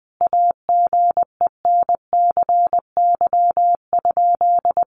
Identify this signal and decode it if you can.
Morse code, heard as a keyed tone, about 20 wpm, 700 Hz. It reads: AGENCY?
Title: Sentences from Top 1300 Words 20wpm